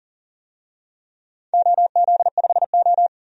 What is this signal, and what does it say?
Morse code, heard as a keyed tone, about 41 wpm, 700 Hz. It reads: OZ5O
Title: DXCC-100 Call Signs - 4 Characters - 40wpm